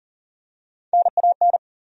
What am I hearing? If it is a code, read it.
Morse code, heard as a keyed tone, about 40 wpm, 700 Hz. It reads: NAN
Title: USA Common Names - Max 3 Letters - 40wpm